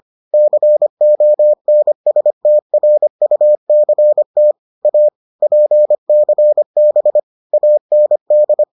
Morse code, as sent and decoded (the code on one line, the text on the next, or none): -.-. --- -. ... - .-. ..- -.-. - / .- / .--. -.-. -... / .- -. -..
CONSTRUCT A PCB AND